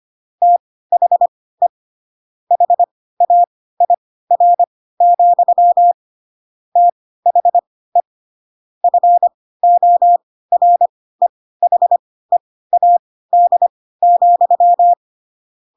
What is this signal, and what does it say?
Morse code, heard as a keyed tone, about 25 wpm, spaced as for 16 wpm, 700 Hz. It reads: THE HAIR, THE FOREHEAD,
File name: Pinocchio / Pinocchio ch3 10 25 ik1hge